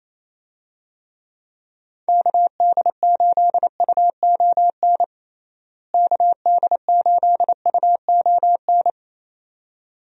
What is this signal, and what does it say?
Morse code, heard as a keyed tone, about 28 wpm, 700 Hz. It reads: KD8UON KD8UON
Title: USA Call Signs - 6 Characters - Repeated 2x - 28wpm